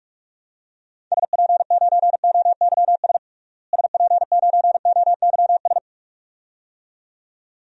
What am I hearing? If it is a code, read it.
Morse code, heard as a keyed tone, about 43 wpm, 700 Hz. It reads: SP9OYS SP9OYS
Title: DXCC-100 Call Signs - 6 Characters - Repeated 2x - 45wpm